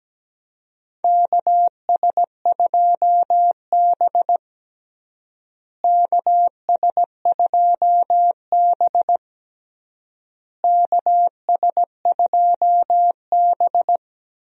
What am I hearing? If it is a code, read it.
Morse code, heard as a keyed tone, about 17 wpm, 700 Hz. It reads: KS2B KS2B KS2B
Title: USA Call Signs - 4 Characters - Repeated 3x - 17wpm